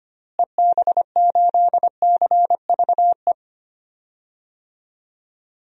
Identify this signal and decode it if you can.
Morse code, heard as a keyed tone, about 25 wpm, 700 Hz. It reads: EB8CVE
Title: DXCC-100 Call Signs - 6 Characters - Repeated 3x - 25wpm